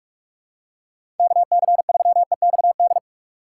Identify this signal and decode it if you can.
Morse code, heard as a keyed tone, about 47 wpm, 700 Hz. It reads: KC3EXD
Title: USA Call Signs - 4x Character Spacing - 45wpm